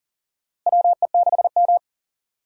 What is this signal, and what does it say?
Morse code, heard as a keyed tone, about 40 wpm, 700 Hz. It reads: WE6M
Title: USA Call Signs - 4 Characters - Repeated 3x - 40wpm